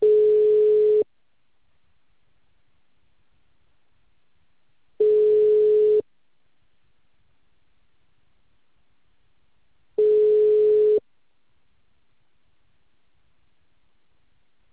Ringback tone (Europe, 425 Hz).